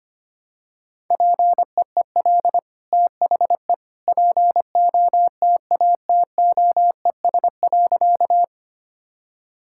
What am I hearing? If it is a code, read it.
Morse code, heard as a keyed tone, about 25 wpm, 700 Hz. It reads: PEEL THE POTATOES.
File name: Easy Sentences with Hints - 25wpm